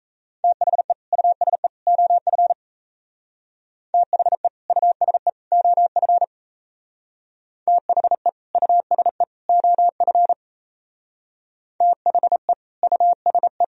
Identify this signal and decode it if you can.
Morse code, heard as a keyed tone, about 34 wpm, 700 Hz. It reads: THE USE OF THE USE OF THE USE OF THE USE
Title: Speed-Racing - Sets of 3 Words 28wpm